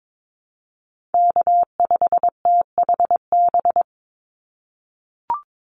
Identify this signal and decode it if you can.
Morse code, heard as a keyed tone, about 22 wpm, 700 Hz. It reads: K5THB
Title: USA Call Signs 22wpm